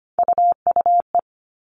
Morse code, heard as a keyed tone, about 25 wpm, 700 Hz. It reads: UUE
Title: All Cut Numbers - 3-Digit 25wpm